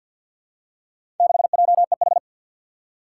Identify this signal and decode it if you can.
Morse code, heard as a keyed tone, about 53 wpm, 700 Hz. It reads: 6JEH
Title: Four Challenging Letter-Number Combinations 50wpm